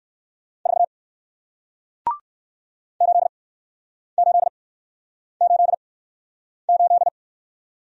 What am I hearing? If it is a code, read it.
Morse code, heard as a keyed tone, about 57 wpm, spaced as for 9 wpm, 700 Hz. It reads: 4 8888